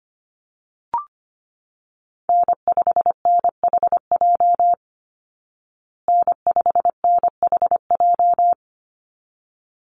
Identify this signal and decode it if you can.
Morse code, heard as a keyed tone, about 25 wpm, 700 Hz. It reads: N5NHJ N5NHJ
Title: Call Signs Repeated 3x - 25wpm